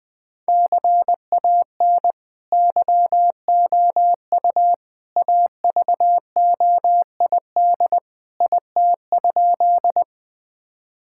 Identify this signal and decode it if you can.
Morse code, heard as a keyed tone, about 20 wpm, 700 Hz. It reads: CAN YOU AVOID IT?
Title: Sentences from Top 1000 Words - Review Part 6 - 20wpm